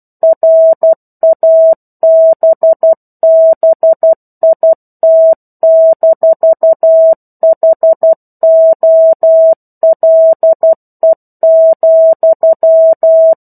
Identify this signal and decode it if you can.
Morse code, heard as a keyed tone, about 12 wpm, 650 Hz. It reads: RABBIT-HOLE,
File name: 01-Chapter-01 12 WPM